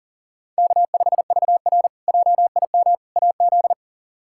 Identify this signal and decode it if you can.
Morse code, heard as a keyed tone, about 40 wpm, 700 Hz. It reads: K5VR JIM AZ
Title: K1USN SST QSO Practice - 40wpm